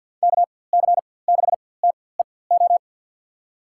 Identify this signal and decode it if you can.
Morse code, heard as a keyed tone, about 50 wpm, spaced as for 23 wpm, 700 Hz. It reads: KC6TEO
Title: USA Call Signs - 4x Character Spacing - 50wpm